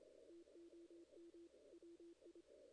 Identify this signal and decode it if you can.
Morse code, heard as a keyed tone, about 25 wpm, 340 Hz. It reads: TOM WI
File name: 2016 JAN CW NAQP / 2016 Jan naqp k5ot - 0-8